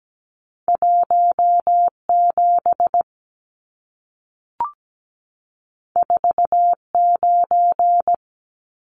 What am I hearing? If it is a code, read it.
Morse code, heard as a keyed tone, about 17 wpm, 700 Hz. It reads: 17 49
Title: Numbers - 2-Digit 17wpm